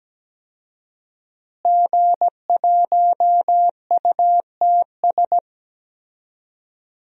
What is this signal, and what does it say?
Morse code, heard as a keyed tone, about 17 wpm, 700 Hz. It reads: G1UTS
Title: DXCC-100 Call Signs - 5 Characters - Repeated 2x - 17wpm